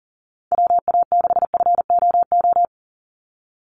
Morse code, heard as a keyed tone, about 40 wpm, 700 Hz. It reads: WA6FOO
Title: USA Call Signs 40wpm